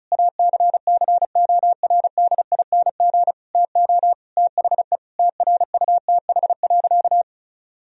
Morse code, heard as a keyed tone, about 36 wpm, 700 Hz. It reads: ACCORDING TO THE TRUTH.
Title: Sentences from Top 1000 Words - Review Part 4 - 35wpm